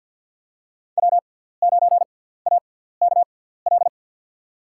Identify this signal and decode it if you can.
Morse code, heard as a keyed tone, about 50 wpm, spaced as for 17 wpm, 700 Hz. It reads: W9AKL
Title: USA Call Signs - 5 Characters - 6x Character Spacing - 50wpm